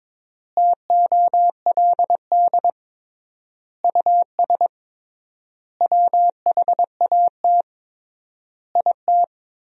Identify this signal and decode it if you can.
Morse code, heard as a keyed tone, about 21 wpm, 700 Hz. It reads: TOLD US WHAT IT
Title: 3x Word-Spacing - Sentences from Top 400 Words - 22wpm